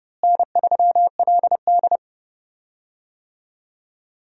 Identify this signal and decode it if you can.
Morse code, heard as a keyed tone, about 30 wpm, 700 Hz. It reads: N3LD
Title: USA Call Signs - 4 Characters - Repeated 2x - 30wpm